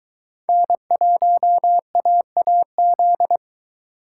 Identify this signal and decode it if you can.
Morse code, heard as a keyed tone, about 22 wpm, 700 Hz. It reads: N1AAZ